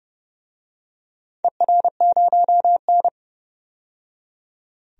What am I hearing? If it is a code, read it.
Morse code, heard as a keyed tone, about 30 wpm, 700 Hz. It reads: ER0N
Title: DXCC-100 Call Signs - 4 Characters - 30wpm